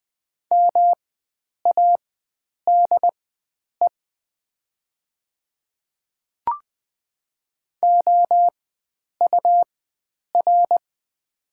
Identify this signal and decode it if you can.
Morse code, heard as a keyed tone, about 20 wpm, spaced as for 9 wpm, 700 Hz. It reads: MADE OUR
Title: 4x Character Spacing - Top 500 Words - 20wpm